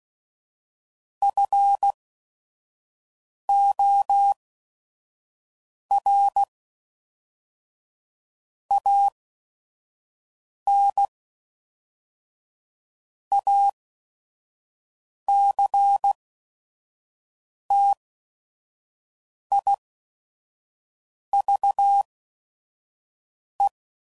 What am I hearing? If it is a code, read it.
Morse code, heard as a keyed tone, about 16 wpm, spaced as for 5 wpm, 800 Hz. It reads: FOR AN ACTIVE
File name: CW Code Groups @5WPM / Technical Subjects / AND Gate